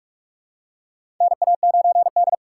Morse code, heard as a keyed tone, about 43 wpm, 700 Hz. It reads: NA9D
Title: USA Call Signs - 4 Characters - 8x Character Spacing - 45wpm